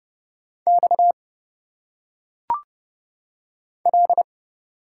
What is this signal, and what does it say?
Morse code, heard as a keyed tone, about 30 wpm, 700 Hz. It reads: X L
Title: Single Letters - Phonetic Pronunciation 30wpm